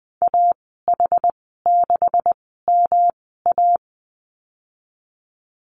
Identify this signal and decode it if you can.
Morse code, heard as a keyed tone, about 20 wpm, spaced as for 15 wpm, 700 Hz. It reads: AH6MA
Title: 167 Lesson 27g - Number 7 - Call Signs 20wpm